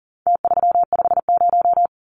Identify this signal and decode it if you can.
Morse code, heard as a keyed tone, about 40 wpm, 700 Hz. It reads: T350
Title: Numbers - Contest Style - 4-Digit 40wpm